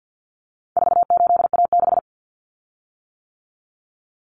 Morse code, heard as a keyed tone, about 50 wpm, 700 Hz. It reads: 48A6
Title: Common Cut Numbers - 4-Digit 50wpm